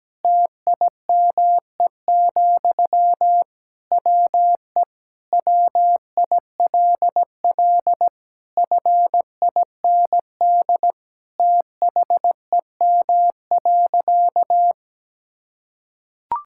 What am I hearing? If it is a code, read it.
Morse code, heard as a keyed tone, about 17 wpm, 700 Hz. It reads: TIME, WE WILL FIND THEM.
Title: Sentences from Top 1000 Words - Review Part 5 - 17wpm